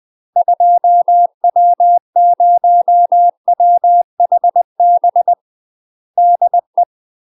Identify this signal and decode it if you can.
Morse code, heard as a keyed tone, about 20 wpm, 700 Hz. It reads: 2W0WHB DE
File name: UK MASTER QSOs TRAINING / 20WPM / UK MASTER QSOs 0070